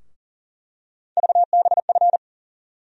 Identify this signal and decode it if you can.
Morse code, heard as a keyed tone, about 40 wpm, 700 Hz. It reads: VBF
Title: USA Call Sign Postfixes - 3 Letters 40wpm